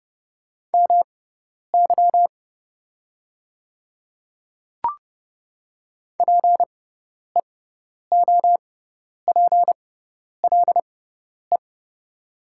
Morse code, heard as a keyed tone, about 29 wpm, spaced as for 10 wpm, 700 Hz. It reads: MY PEOPLE
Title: 6x Character Spacing - Top 100 Words - 30wpm